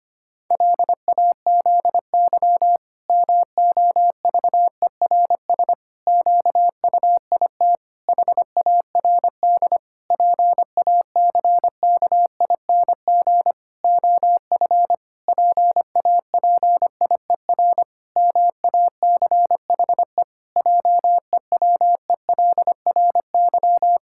Morse code, heard as a keyed tone, about 25 wpm, 700 Hz. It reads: LAZY MOVERS QUIT HARD PACKING OF PAPIER MACHE JEWELRY
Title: Pangram Challenge v2 - 25wpm